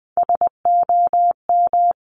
Morse code, heard as a keyed tone, about 20 wpm, 700 Hz. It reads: SOM
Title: Top Three Letter Combinations in Words 20wpm